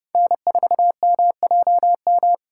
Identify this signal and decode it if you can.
Morse code, heard as a keyed tone, about 30 wpm, 700 Hz. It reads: N4MJM
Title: USA Call Signs - 5 Characters - 3x Character Spacing - 30wpm